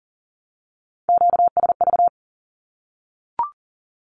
Morse code, heard as a keyed tone, about 40 wpm, 700 Hz. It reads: QSV
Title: Q-Codes Repeated 3x 40wpm